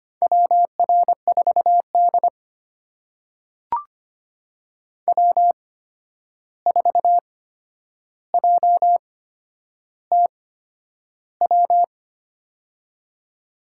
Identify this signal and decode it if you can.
Morse code, heard as a keyed tone, about 25 wpm, spaced as for 10 wpm, 700 Hz. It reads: WR4D W4JTW
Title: USA Call Signs - 8x Character Spacing - 25wpm